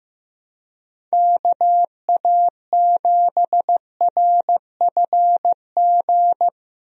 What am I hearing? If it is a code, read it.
Morse code, heard as a keyed tone, about 15 wpm, 700 Hz. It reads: KA7RFG
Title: USA Call Signs - 6 Characters - Repeated 2x - 15wpm